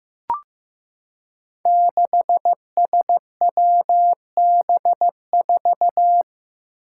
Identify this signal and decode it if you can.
Morse code, heard as a keyed tone, about 15 wpm, 700 Hz. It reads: 6SWB4
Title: Five Challenging Letter-Number Combinations 15wpm